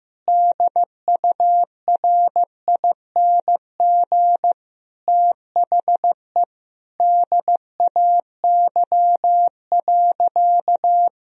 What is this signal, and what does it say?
Morse code, heard as a keyed tone, about 16 wpm, 700 Hz. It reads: DURING THE DAY.
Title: Sentences from Top 1000 Words - Review Part 3 - 15wpm